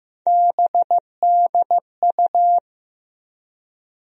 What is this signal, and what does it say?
Morse code, heard as a keyed tone, about 16 wpm, 700 Hz. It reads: BDU
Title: USA Call Sign Postfixes - 3 Letters 15wpm